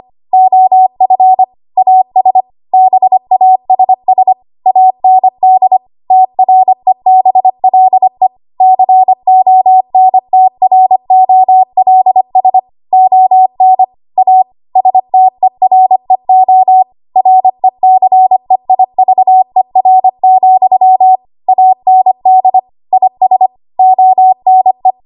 Morse code, heard as a keyed tone, about 25 wpm, 750 Hz. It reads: OF AS BASS AND TREBLE CONTROLS ON A STEREO RECEIVER, AND IS ONE
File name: Morse / 25WPM